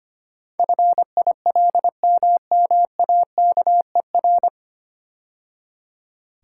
Morse code, heard as a keyed tone, about 25 wpm, 700 Hz. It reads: FILMMAKER
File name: practice / 25wpm / 9 Letter Words - First 3 Letters Given - 25wpm